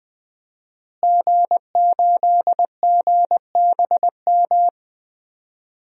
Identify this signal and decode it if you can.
Morse code, heard as a keyed tone, about 20 wpm, 700 Hz. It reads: G8GBM
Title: DXCC-100 Call Signs - 5 Characters - Repeated 2x - 20wpm